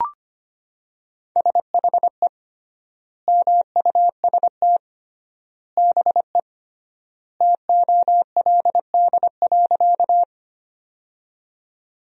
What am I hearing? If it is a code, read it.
Morse code, heard as a keyed tone, about 25 wpm, 700 Hz. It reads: SHE MUST BE TOLD.
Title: 3x Word-Spacing - Sentences from Top 400 Words - 25wpm